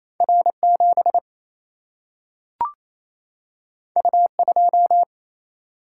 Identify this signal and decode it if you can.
Morse code, heard as a keyed tone, about 28 wpm, 700 Hz. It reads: R7 U2
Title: Single-Letter Single-Number 28wpm